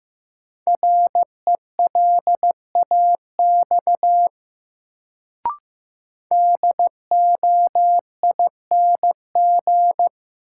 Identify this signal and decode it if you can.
Morse code, heard as a keyed tone, about 15 wpm, 700 Hz. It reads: RELAX DOING
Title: Sentences from Top 1400 Words - New Word Drill Alt - 15wpm